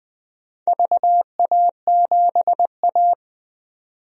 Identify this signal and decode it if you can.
Morse code, heard as a keyed tone, about 20 wpm, 700 Hz. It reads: VA7A
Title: DXCC-100 Call Signs - 4 Characters - 20wpm